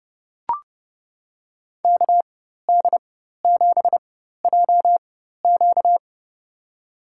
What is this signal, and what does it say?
Morse code, heard as a keyed tone, about 30 wpm, spaced as for 14 wpm, 700 Hz. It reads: KD7JQ
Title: USA Call Signs - 5 Characters - 4x Character Spacing - 30wpm